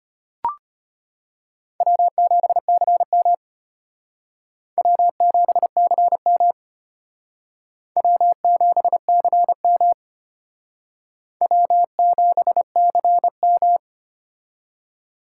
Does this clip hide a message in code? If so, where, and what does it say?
Yes: Morse code from 1.8 to 13.8 s, heard as a keyed tone, about 32 wpm, 700 Hz. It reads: W7CM W7CM W7CM W7CM